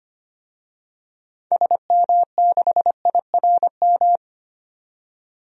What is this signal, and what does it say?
Morse code, heard as a keyed tone, about 25 wpm, 700 Hz. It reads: SM6IRM
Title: DXCC-100 Call Signs - 6 Characters - 25wpm